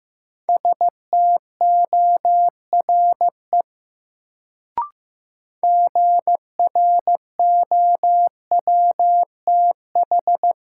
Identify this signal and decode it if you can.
Morse code, heard as a keyed tone, about 15 wpm, 700 Hz. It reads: STORE GROWTH
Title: Sentences from Top 1000 Words - Review Part 5 - New Word Drill - 15wpm